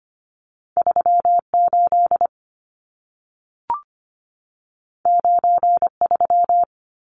Morse code, heard as a keyed tone, about 25 wpm, 700 Hz. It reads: 38 93
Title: Numbers - 2-Digit 25wpm